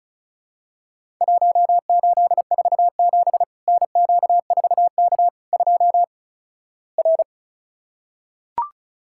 Morse code, heard as a keyed tone, about 36 wpm, 700 Hz. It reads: 1847 NQ4K 2 R
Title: Worked All Europe DX Contest - QTC Exchange - 35wpm